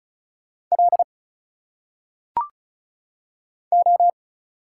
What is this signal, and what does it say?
Morse code, heard as a keyed tone, about 36 wpm, 700 Hz. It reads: L O
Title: Single Letters - Phonetic Pronunciation 35wpm